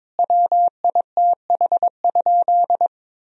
Morse code, heard as a keyed tone, about 22 wpm, 700 Hz. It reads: WITH?
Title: Sentences from Top 1000 Words - Review Part 1 - 22wpm